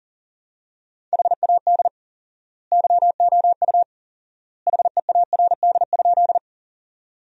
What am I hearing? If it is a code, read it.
Morse code, heard as a keyed tone, about 40 wpm, 700 Hz. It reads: HAD YOU HEARD?
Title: 4x Word-Spacing - Sentences from Top 400 Words - 40wpm